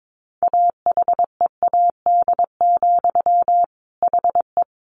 Morse code, heard as a keyed tone, about 22 wpm, 700 Hz. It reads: AHEAD, HE
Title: A Sound Of Thunder 22wpm